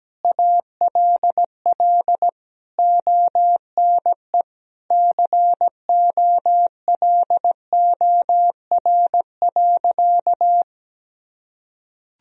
Morse code, heard as a keyed tone, about 17 wpm, 700 Hz. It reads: ALL ONE COLOR.
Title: Sentences from Top 1000 Words - Review Part 2 - 17wpm